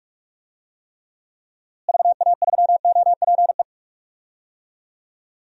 Morse code, heard as a keyed tone, about 44 wpm, 700 Hz. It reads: VA3OPE
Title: DXCC-100 Call Signs - 6 Characters - Repeated 2x - 45wpm